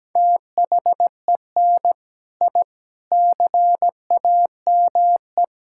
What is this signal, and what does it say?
Morse code, heard as a keyed tone, about 17 wpm, 700 Hz. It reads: THEN I CAME
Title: Sentences from Top 1000 Words - Review Part 6 - 17wpm